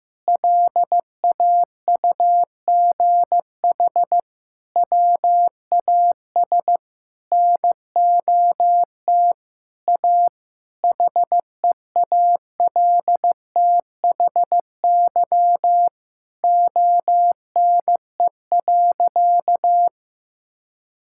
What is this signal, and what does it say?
Morse code, heard as a keyed tone, about 16 wpm, 700 Hz. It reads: LAUGH WAS NOT A HEALTHY ONE.
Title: Sentences from Top 1000 Words - Review Part 5 - 15wpm